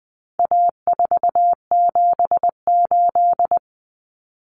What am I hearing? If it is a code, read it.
Morse code, heard as a keyed tone, about 20 wpm, 700 Hz. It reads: A478
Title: Common Cut Numbers - 4-Digit 20wpm